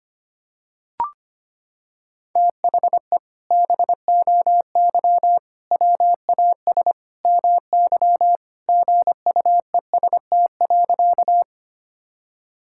Morse code, heard as a keyed tone, about 25 wpm, 700 Hz. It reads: THE BOY WAS MY GUEST.
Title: Sentences from Top 1600 Words 25wpm